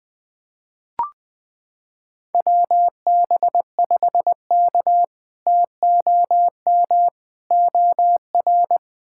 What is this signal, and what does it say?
Morse code, heard as a keyed tone, about 20 wpm, 700 Hz. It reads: WB5K TOM OR
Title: K1USN SST QSO Practice - 20wpm